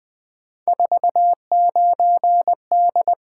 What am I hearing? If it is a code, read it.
Morse code, heard as a keyed tone, about 20 wpm, 700 Hz. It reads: 49D
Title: Three Letter-Number Combinations - 20wpm